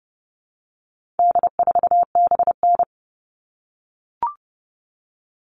Morse code, heard as a keyed tone, about 30 wpm, 700 Hz. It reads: D4BN
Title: All Cut Numbers - 4-Digit 30wpm